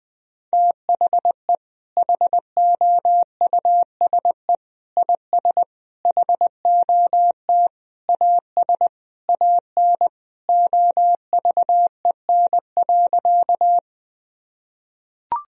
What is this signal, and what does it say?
Morse code, heard as a keyed tone, about 20 wpm, 700 Hz. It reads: THE HOUSE IS HOT AS AN OVEN.
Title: Sentences from Top 1000 Words - Review Part 4 - 20wpm